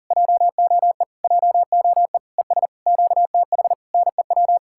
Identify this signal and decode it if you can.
Morse code, heard as a keyed tone, about 57 wpm, spaced as for 49 wpm, 700 Hz. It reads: JOE JOE ES QTH NEW
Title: Speed-Racing Common QSO Phrases Encore - 40wpm